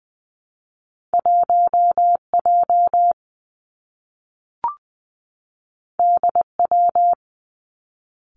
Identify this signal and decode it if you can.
Morse code, heard as a keyed tone, about 20 wpm, 700 Hz. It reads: 1 J D W
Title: Two Challenging Letter-Number Combinations 20wpm